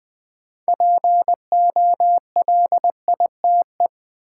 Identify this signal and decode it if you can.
Morse code, heard as a keyed tone, about 20 wpm, 700 Hz. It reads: POLITE
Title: 6 Letter Words - First 3 Letters Given - 20wpm